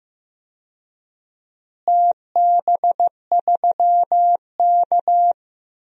Morse code, heard as a keyed tone, about 15 wpm, 700 Hz. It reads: TB3K
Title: DXCC-100 Call Signs - 4 Characters - Repeated 3x - 15wpm